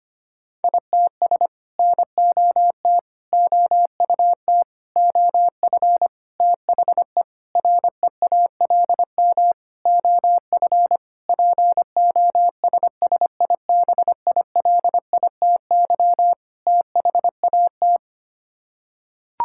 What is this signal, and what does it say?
Morse code, heard as a keyed tone, about 25 wpm, 700 Hz. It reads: ITS NOT OUT OF THE REALM OF POSSIBILITY THAT
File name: Sets of 9 Words - 25wpm